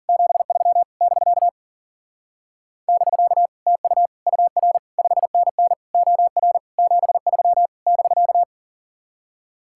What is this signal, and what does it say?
Morse code, heard as a keyed tone, about 52 wpm, 700 Hz. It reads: 73 <BK> <BK> TU UR 5NN OR 73 <BK>